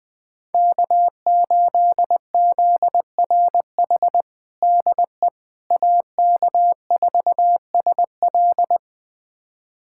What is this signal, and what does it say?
Morse code, heard as a keyed tone, about 20 wpm, 700 Hz. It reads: K8ZRH DE AK4SL
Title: USA Callsign DE Callsign - 5 Characters - 20wpm